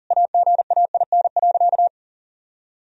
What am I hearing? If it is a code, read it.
Morse code, heard as a keyed tone, about 40 wpm, 700 Hz. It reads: AGAIN.
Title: Sentences from Top 1000 Words - Review Part 2 - 40wpm